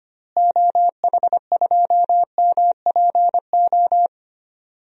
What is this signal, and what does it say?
Morse code, heard as a keyed tone, about 25 wpm, 700 Hz. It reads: OH2MPO
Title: DXCC-100 Call Signs - 6 Characters - Repeated 3x - 25wpm